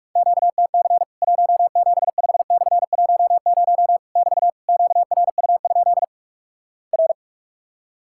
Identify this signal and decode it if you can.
Morse code, heard as a keyed tone, about 46 wpm, 700 Hz. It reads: QTC 175/10 <BT> QRU? R
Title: Worked All Europe DX Contest - QTC Exchange - 45wpm